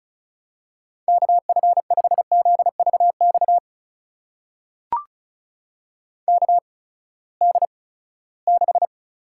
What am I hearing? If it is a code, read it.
Morse code, heard as a keyed tone, about 34 wpm, spaced as for 20 wpm, 700 Hz. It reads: KF5ZVX KD6
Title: USA Call Signs - 6 Characters - 8x Character Spacing - 35wpm